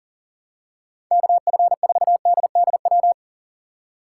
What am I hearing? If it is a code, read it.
Morse code, heard as a keyed tone, about 37 wpm, 700 Hz. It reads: KF4DDW
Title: USA Call Signs - 6 Characters - 2x Character Spacing - 40wpm